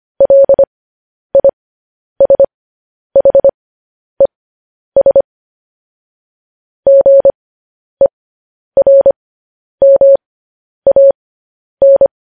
Morse code, heard as a keyed tone, about 25 wpm, spaced as for 10 wpm, 550 Hz. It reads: LISHES GERMAN